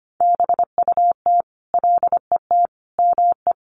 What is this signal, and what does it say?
Morse code, heard as a keyed tone, about 25 wpm, 700 Hz. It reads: BUT LET ME
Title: Sentences from Top 1300 Words 25wpm